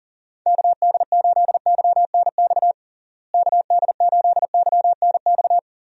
Morse code, heard as a keyed tone, about 40 wpm, 700 Hz. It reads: KD8YNX KD8YNX
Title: USA Call Signs - 6 Characters - Repeated 2x - 40wpm